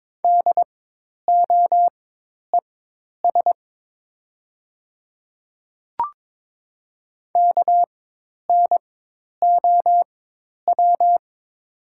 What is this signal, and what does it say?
Morse code, heard as a keyed tone, about 22 wpm, spaced as for 10 wpm, 700 Hz. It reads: DOES KNOW
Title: 4x Character Spacing - Top 200 Words - 22wpm